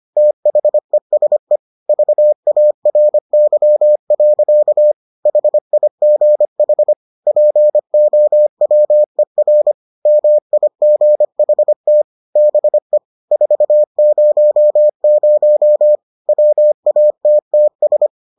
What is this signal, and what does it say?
Morse code, heard as a keyed tone, about 25 wpm, 600 Hz. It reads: THESE VARY. HIGH POWER MIGHT BE 400 WATTS